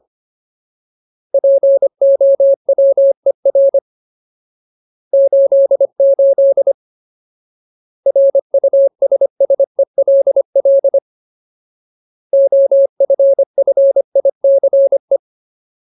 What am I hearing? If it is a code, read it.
Morse code, heard as a keyed tone, about 25 wpm, 550 Hz. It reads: POWER 88 RUSSELL OFFICE
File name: Ultimate Word List / 25WPM Ultimate